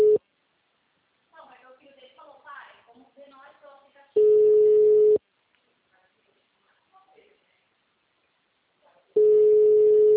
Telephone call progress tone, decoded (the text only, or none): ringback tone